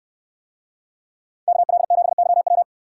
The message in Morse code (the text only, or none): KL7YW